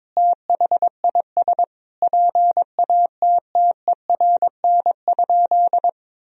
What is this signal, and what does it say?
Morse code, heard as a keyed tone, about 22 wpm, 700 Hz. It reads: THIS PATTERN?
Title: Sentences from Top 1000 Words - Review Part 2 - 22wpm